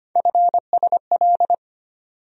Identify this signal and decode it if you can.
Morse code, heard as a keyed tone, about 25 wpm, 700 Hz. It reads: FSL
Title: USA Call Sign Postfixes - 3 Letters 25wpm